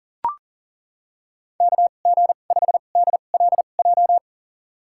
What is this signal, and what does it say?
Morse code, heard as a keyed tone, about 39 wpm, spaced as for 29 wpm, 700 Hz. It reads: KG5DLJ
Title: USA Call Signs - 6 Characters - 2x Character Spacing - 40wpm